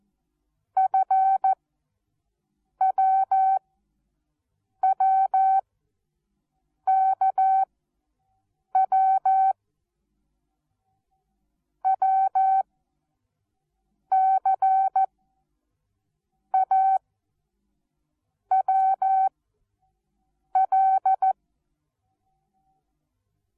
Morse code, heard as a keyed tone, about 14 wpm, spaced as for 5 wpm, 770 Hz. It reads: FWWKW WCAWL
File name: Onlinelehrgang / morsen / DJ4UF Morsen19